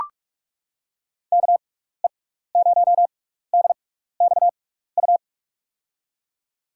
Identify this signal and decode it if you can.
Morse code, heard as a keyed tone, about 44 wpm, spaced as for 15 wpm, 700 Hz. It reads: KE0DXU
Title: USA Call Signs - 6 Characters - 6x Character Spacing - 45wpm